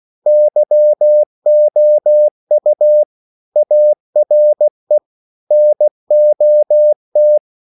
Morse code, heard as a keyed tone, about 16 wpm, 600 Hz. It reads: YOU ARE NOT
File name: CW / Practice / EnglishQuotations0116